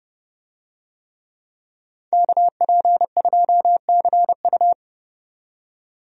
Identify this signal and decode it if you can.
Morse code, heard as a keyed tone, about 30 wpm, 700 Hz. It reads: KP2CU
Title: DXCC-100 Call Signs - 5 Characters - 30wpm